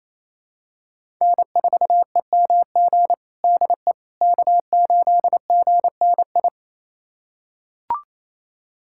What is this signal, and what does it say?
Morse code, heard as a keyed tone, about 28 wpm, 700 Hz. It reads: N4EMG DE K8GNI
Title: USA Callsign DE Callsign - 5 Characters - 28wpm